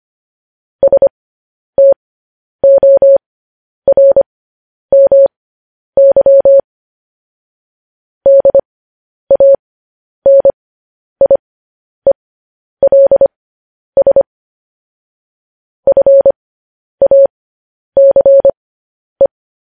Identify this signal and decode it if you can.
Morse code, heard as a keyed tone, about 25 wpm, spaced as for 10 wpm, 550 Hz. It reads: STORMY DANIELS FACE